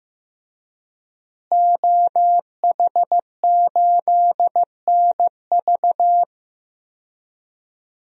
Morse code, heard as a keyed tone, about 16 wpm, 700 Hz. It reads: OH8NV
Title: DXCC-100 Call Signs - 5 Characters - Repeated 3x - 15wpm